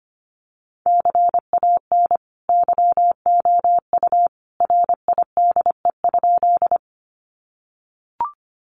Morse code, heard as a keyed tone, about 25 wpm, 700 Hz. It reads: CAN YOU RIDE?
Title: Sentences from Top 1200 Words 25wpm